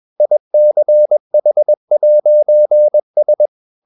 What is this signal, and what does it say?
Morse code, heard as a keyed tone, about 21 wpm, 600 Hz. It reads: ICH'S